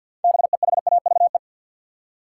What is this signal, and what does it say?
Morse code, heard as a keyed tone, about 47 wpm, 700 Hz. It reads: BEHAVE
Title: Six Letter Words - 50wpm